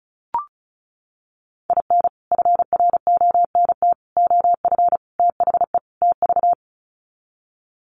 Morse code, heard as a keyed tone, about 34 wpm, 700 Hz. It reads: IN FRONT OF THE TV
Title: Sets of 5 Words Encore2 35wpm